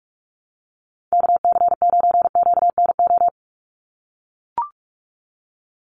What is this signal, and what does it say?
Morse code, heard as a keyed tone, about 44 wpm, 700 Hz. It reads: KC9QNO
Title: USA Call Signs 45wpm